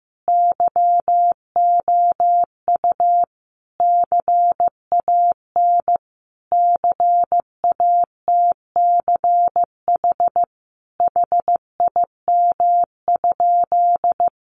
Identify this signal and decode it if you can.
Morse code, heard as a keyed tone, about 15 wpm, 700 Hz. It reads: YOU CAN CATCH HIM?
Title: Sentences from Top 1500 Words 15wpm